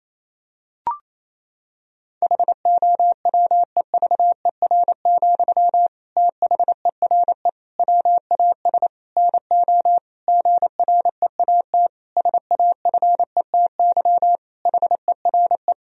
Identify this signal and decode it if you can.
Morse code, heard as a keyed tone, about 28 wpm, 700 Hz. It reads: HOWEVER, THERE WAS NO GREAT SAFETY HERE
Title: Sentences from Top 1000 Words - Review Part 4 - 28wpm